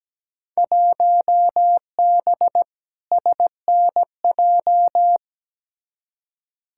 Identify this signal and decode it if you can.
Morse code, heard as a keyed tone, about 17 wpm, 700 Hz. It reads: 1B SNJ
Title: Field Day Exchanges Encore - 17wpm